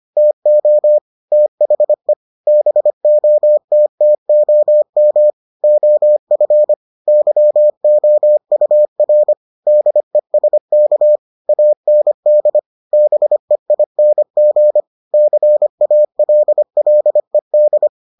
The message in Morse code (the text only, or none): TO THE BOTTOM OF YOUR DESK AND BEING CALLED